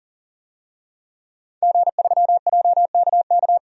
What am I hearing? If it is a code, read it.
Morse code, heard as a keyed tone, about 40 wpm, 700 Hz. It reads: G3JKK
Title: DXCC-100 Call Signs - 5 Characters - 40wpm